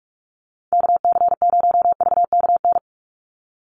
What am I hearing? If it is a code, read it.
Morse code, heard as a keyed tone, about 44 wpm, 700 Hz. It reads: KC0VKN
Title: Call Signs Repeated 3x - 45wpm